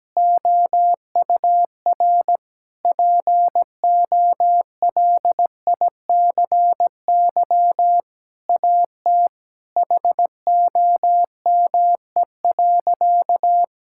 Morse code, heard as a keyed tone, about 17 wpm, 700 Hz. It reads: OUR POLICY AT HOME.
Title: Sentences from Top 1000 Words - Review Part 5 - 17wpm